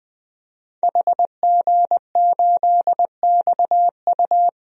Morse code, heard as a keyed tone, about 20 wpm, 700 Hz. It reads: HG8XU
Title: DXCC-100 Call Signs - 5 Characters - Repeated 3x - 20wpm